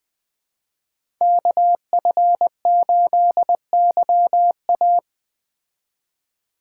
Morse code, heard as a keyed tone, about 19 wpm, 700 Hz. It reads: KF8YA
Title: Speed Racing - USA Call Signs 17wpm